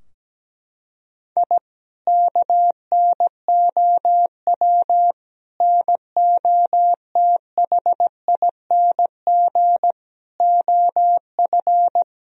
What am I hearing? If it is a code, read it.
Morse code, heard as a keyed tone, about 17 wpm, 700 Hz. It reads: I KNOW NOTHING OF
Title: Sentences from Top 1000 Words - Review Part 4 - 17wpm